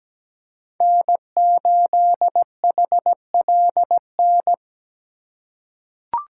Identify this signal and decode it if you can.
Morse code, heard as a keyed tone, about 17 wpm, 700 Hz. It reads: N8HLN
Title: USA Call Signs - 5 Characters - 4x Character Spacing - 17wpm